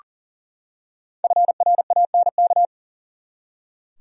Morse code, heard as a keyed tone, about 40 wpm, 700 Hz. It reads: FRANK
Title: USA Common Names - 40wpm